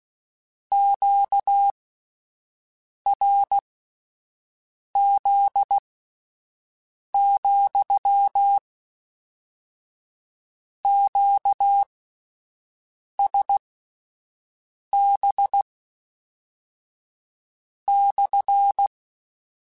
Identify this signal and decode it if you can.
Morse code, heard as a keyed tone, about 16 wpm, spaced as for 5 wpm, 800 Hz. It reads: QRZ, QSB /